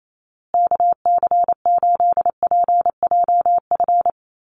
Morse code, heard as a keyed tone, about 28 wpm, 700 Hz. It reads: KC8PJF
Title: USA Call Signs Repeated 3x 28wpm